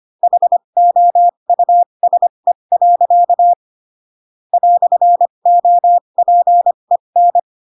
Morse code, heard as a keyed tone, about 25 wpm, spaced as for 22 wpm, 700 Hz. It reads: HOUSE. "OPEN
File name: Pinocchio / Pinocchio ch7 18 25 ik1hge